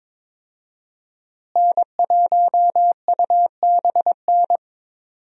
Morse code, heard as a keyed tone, about 22 wpm, 700 Hz. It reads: N1UBN
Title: USA Call Signs - 5 Characters - Repeated 2x - 22wpm